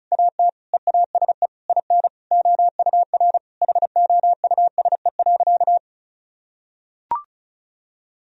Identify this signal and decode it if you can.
Morse code, heard as a keyed tone, about 36 wpm, 700 Hz. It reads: AT EASE IN OUR HOUSE.
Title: Sentences from Top 1000 Words - Review Part 1 - 35wpm